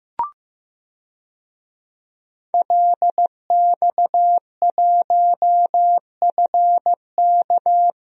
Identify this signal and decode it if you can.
Morse code, heard as a keyed tone, about 15 wpm, 700 Hz. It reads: LX1FK
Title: DXCC-100 Call Signs - 5 Characters - Repeated 2x - 15wpm